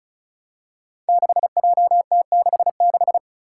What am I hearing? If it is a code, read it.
Morse code, heard as a keyed tone, about 36 wpm, 700 Hz. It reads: 6JT66
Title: Five Challenging Letter-Number Combinations 35wpm